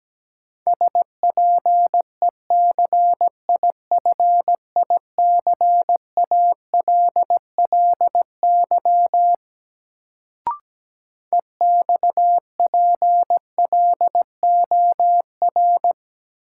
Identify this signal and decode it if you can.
Morse code, heard as a keyed tone, about 17 wpm, 700 Hz. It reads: SPECIFICALLY EXPLOR
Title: Sentences from Top 1100 Words - New Word Drill - 17wpm